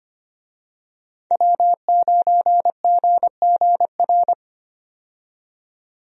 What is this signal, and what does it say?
Morse code, heard as a keyed tone, about 25 wpm, 700 Hz. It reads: W9GGR
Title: USA Call Signs - 5 Characters - 25wpm